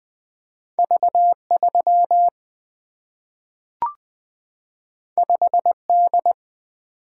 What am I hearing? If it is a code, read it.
Morse code, heard as a keyed tone, about 20 wpm, 700 Hz. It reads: V3 5D
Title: Single-Letter Single-Number 20wpm